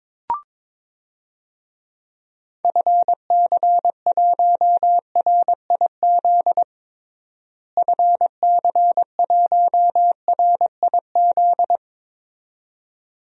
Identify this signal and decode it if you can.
Morse code, heard as a keyed tone, about 22 wpm, 700 Hz. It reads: FC1RIZ FC1RIZ
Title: DXCC-100 Call Signs - 6 Characters - Repeated 2x - 22wpm